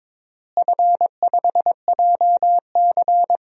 Morse code, heard as a keyed tone, about 22 wpm, 700 Hz. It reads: F5JC
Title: DXCC-100 Call Signs - 4 Characters - Repeated 3x - 22wpm